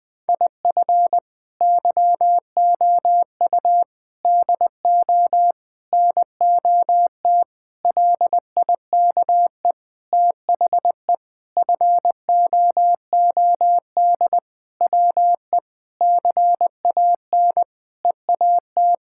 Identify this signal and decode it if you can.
Morse code, heard as a keyed tone, about 21 wpm, 700 Hz. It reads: IF YOU DO NOT LIKE THE FOOD WE CAN EAT
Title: Sentences with 5-Word Phrases - 20wpm